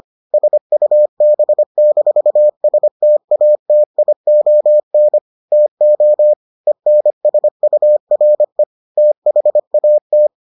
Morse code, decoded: SUB-STATION TO ENSURE THAT

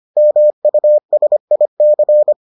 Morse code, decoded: MUSIC

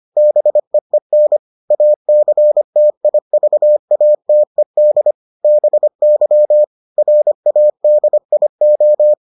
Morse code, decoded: BEEN ACTIVATED BY RADIO

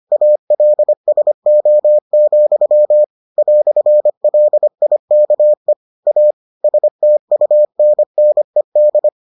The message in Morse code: ALSO, "LIKE A STUNNED

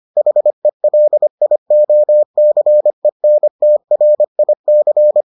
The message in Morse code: HELIOCENTRIC

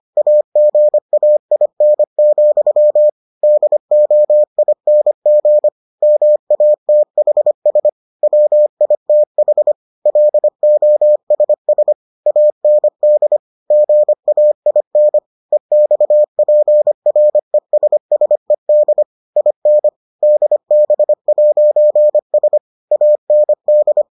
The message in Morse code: AGAIN, DOING MATHS WITH LOSS AND GAIN EXPRESSED IN DB'S AND

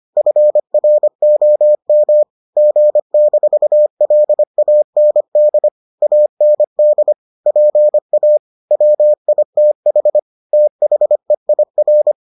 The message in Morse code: FROM G-LAND AND PA WITH THEIR

